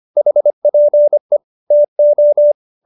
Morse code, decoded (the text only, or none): HPE TO